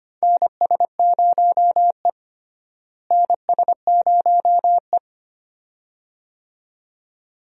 Morse code, heard as a keyed tone, about 25 wpm, 700 Hz. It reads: NS0E NS0E
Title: USA Call Signs - 4 Characters - Repeated 2x - 25wpm